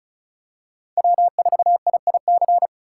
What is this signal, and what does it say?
Morse code, heard as a keyed tone, about 36 wpm, 700 Hz. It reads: W4IIC
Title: K1USN SST QSO Practice - 35wpm